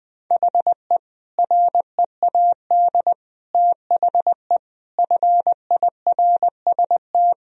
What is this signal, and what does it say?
Morse code, heard as a keyed tone, about 20 wpm, 700 Hz. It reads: HE READ THE FIRST
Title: Sentences from Top 1000 Words - Review Part 1 - 20wpm